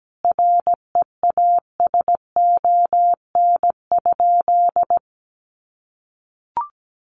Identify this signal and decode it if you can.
Morse code, heard as a keyed tone, about 17 wpm, 700 Hz. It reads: REASON?
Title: Sentences from Top 700 Words 17wpm